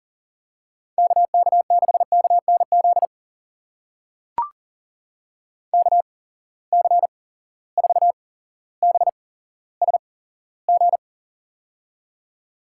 Morse code, heard as a keyed tone, about 40 wpm, spaced as for 17 wpm, 700 Hz. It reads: KK6KNZ KC4BSG
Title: USA Call Signs - 6 Characters - 8x Character Spacing - 40wpm